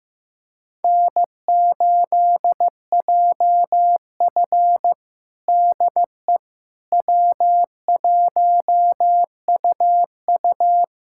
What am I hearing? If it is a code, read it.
Morse code, heard as a keyed tone, about 15 wpm, 700 Hz. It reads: N8JF DE W1UU
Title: USA Callsign DE Callsign - 4 Characters - 15wpm